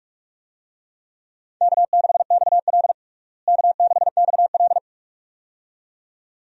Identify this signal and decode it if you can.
Morse code, heard as a keyed tone, about 41 wpm, 700 Hz. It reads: K6XL K6XL
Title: USA Call Signs - 4 Characters - Repeated 2x - 45wpm